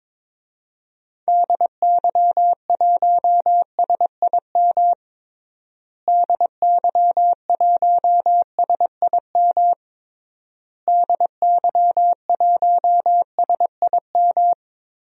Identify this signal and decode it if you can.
Morse code, heard as a keyed tone, about 22 wpm, 700 Hz. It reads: DY1SIM DY1SIM DY1SIM
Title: DXCC-100 Call Signs - 6 Characters - Repeated 3x - 22wpm